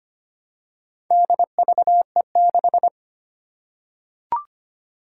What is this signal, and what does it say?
Morse code, heard as a keyed tone, about 25 wpm, 700 Hz. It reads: DVE6
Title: Four Challenging Letter-Number Combinations 25wpm